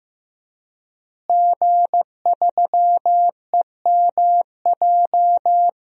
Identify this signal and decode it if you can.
Morse code, heard as a keyed tone, about 15 wpm, 700 Hz. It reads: G3EMJ
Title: DXCC-100 Call Signs - 5 Characters - Repeated 3x - 15wpm